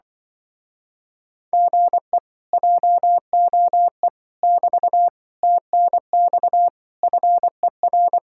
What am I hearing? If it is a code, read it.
Morse code, heard as a keyed tone, about 24 wpm, 700 Hz. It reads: GE JOE <BT> TNX FER
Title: Speed-Racing Common QSO Phrases - 20wpm